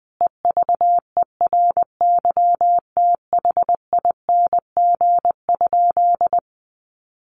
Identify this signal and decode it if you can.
Morse code, heard as a keyed tone, about 20 wpm, 700 Hz. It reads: EVERYTHING?
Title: Sentences from Top 1100 Words 20wpm